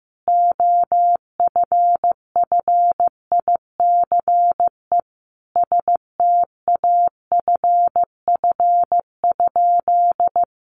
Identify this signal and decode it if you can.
Morse code, heard as a keyed tone, about 15 wpm, 700 Hz. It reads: OFFICE STAFF?
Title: Sentences from Top 1200 Words 15wpm